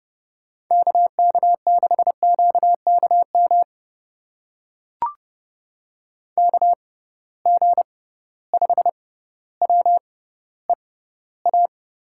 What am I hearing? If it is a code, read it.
Morse code, heard as a keyed tone, about 31 wpm, spaced as for 15 wpm, 700 Hz. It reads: KK6QKM KG5WEA
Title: USA Call Signs - 6 Characters - 6x Character Spacing - 30wpm